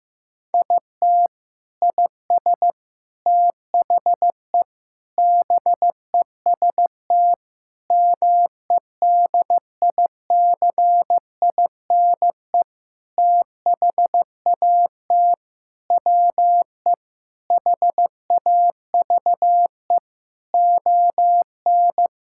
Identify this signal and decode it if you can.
Morse code, heard as a keyed tone, about 16 wpm, 700 Hz. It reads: IT IS THE BEST MEDICINE THAT WE HAVE ON
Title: Sentences from Top 1000 Words - Review Part 6 - 15wpm